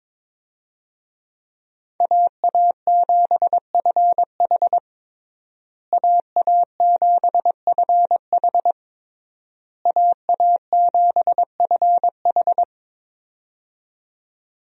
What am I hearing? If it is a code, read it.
Morse code, heard as a keyed tone, about 22 wpm, 700 Hz. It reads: AA7FH AA7FH AA7FH
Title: USA Call Signs - 5 Characters - Repeated 3x - 22wpm